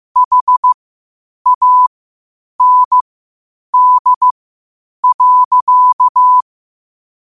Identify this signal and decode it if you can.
Morse code, heard as a keyed tone, about 15 wpm, spaced as for 8 wpm, 1000 Hz. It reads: HAND.
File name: qotdfiles / 8wpm